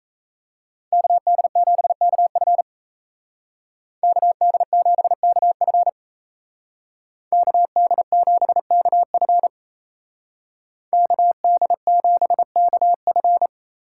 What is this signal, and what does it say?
Morse code, heard as a keyed tone, about 35 wpm, 700 Hz. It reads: KD7KF KD7KF KD7KF KD7KF
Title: Speed Racing - USA Call Signs 28wpm